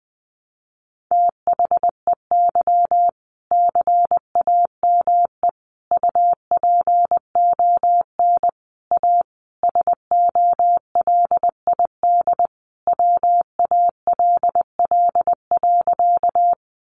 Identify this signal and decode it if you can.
Morse code, heard as a keyed tone, about 20 wpm, 700 Hz. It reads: THEY CAME UPON A SOLID WALL.
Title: Sentences from Top 1500 Words 20wpm